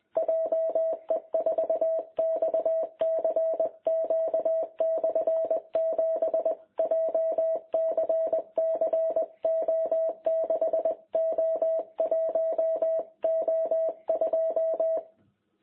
Morse code, heard as a keyed tone, about 21 wpm, 660 Hz. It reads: JE4XCQ/7 JCC O6O1O2
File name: jh8cbh / 2021cw